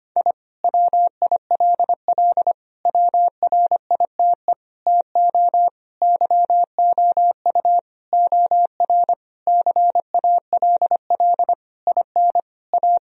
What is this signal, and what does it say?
Morse code, heard as a keyed tone, about 25 wpm, 700 Hz. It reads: I WILL WRITE TO YOU OR CALL IN A
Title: Sentences from Top 1000 Words - Review Part 1 - 25wpm